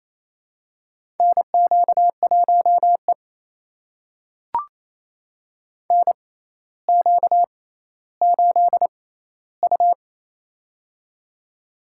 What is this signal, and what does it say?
Morse code, heard as a keyed tone, about 28 wpm, spaced as for 14 wpm, 700 Hz. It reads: NQ1E NQ8U
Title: USA Call Signs - 4 Characters - 6x Character Spacing - 28wpm